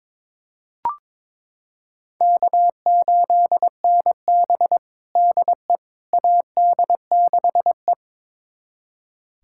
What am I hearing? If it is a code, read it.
Morse code, heard as a keyed tone, about 22 wpm, 700 Hz. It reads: K8NB DE AD6E
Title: USA Callsign DE Callsign - 4 Characters - 22wpm